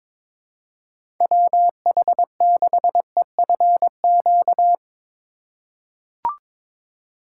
Morse code, heard as a keyed tone, about 21 wpm, 700 Hz. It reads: WH6EFQ
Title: USA Call Signs - 3x Character Spacing - 22wpm